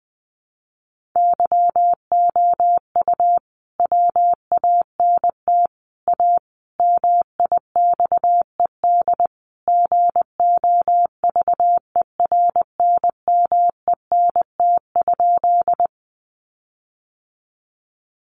Morse code, heard as a keyed tone, about 20 wpm, 700 Hz. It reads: YOU WANT A MIXED GOVERNMENT?
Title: Sentences from Top 1500 Words 20wpm